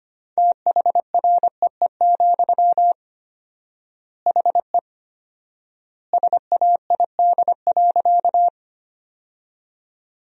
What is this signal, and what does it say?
Morse code, heard as a keyed tone, about 25 wpm, 700 Hz. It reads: THREE, HE SAID.
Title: 4x Word-Spacing - Sentences from Top 200 Words - 25wpm